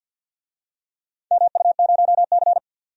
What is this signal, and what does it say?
Morse code, heard as a keyed tone, about 51 wpm, 700 Hz. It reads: MU0C
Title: DXCC-100 Call Signs - 4 Characters - 50wpm